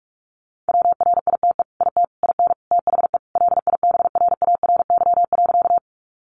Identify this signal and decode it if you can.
Morse code, heard as a keyed tone, about 44 wpm, 700 Hz. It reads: WRITE IT IN THE LIBRARY.
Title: Sentences from Top 600 Words 45wpm